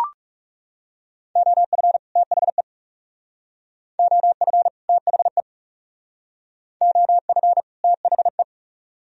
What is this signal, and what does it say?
Morse code, heard as a keyed tone, about 36 wpm, 700 Hz. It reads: OF THE OF THE OF THE
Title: Speed-Racing - Sets of 2 Words 30wpm